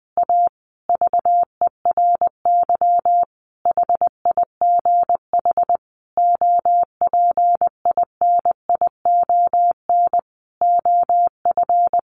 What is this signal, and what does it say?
Morse code, heard as a keyed tone, about 20 wpm, 700 Hz. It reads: A VERY HIGH OPINION OF
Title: Sentences from Top 1300 Words 20wpm